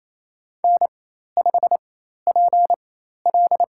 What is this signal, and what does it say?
Morse code, heard as a keyed tone, about 28 wpm, spaced as for 13 wpm, 700 Hz. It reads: N5PL
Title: USA Call Signs - 4 Characters - 4x Character Spacing - 28wpm